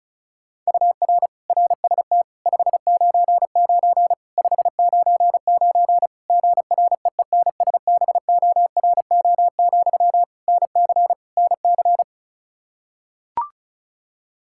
Morse code, heard as a keyed tone, about 36 wpm, 700 Hz. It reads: UR RST 599 599 GREENSBORO, NC NC
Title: Practice QSO - Exchange 1 - Beginning - 35wpm